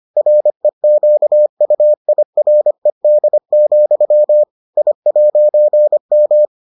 Morse code, heard as a keyed tone, about 25 wpm, 600 Hz. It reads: REQUIRED, I'M